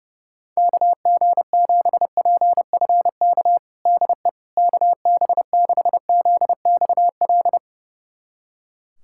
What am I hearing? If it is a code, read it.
Morse code, heard as a keyed tone, about 29 wpm, 700 Hz. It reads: KG7PFK DE KB6ZXL
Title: USA Callsign DE Callsign - 6 Characters - 30wpm